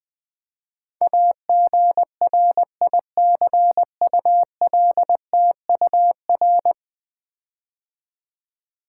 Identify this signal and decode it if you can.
Morse code, heard as a keyed tone, about 20 wpm, 700 Hz. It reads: AGRICULTUR
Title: Build A Word - 5 Letters At A Time - 12-Letter Words - 20wpm